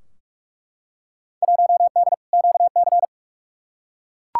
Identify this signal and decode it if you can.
Morse code, heard as a keyed tone, about 42 wpm, 700 Hz. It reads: 1D QC
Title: Field Day Exchanges Encore - 45wpm